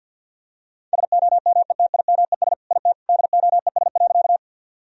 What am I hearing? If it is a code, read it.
Morse code, heard as a keyed tone, about 50 wpm, 700 Hz. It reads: SOMETIMES IT DOES.
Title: Sentences from Top 1000 Words - Review Part 3 - 50wpm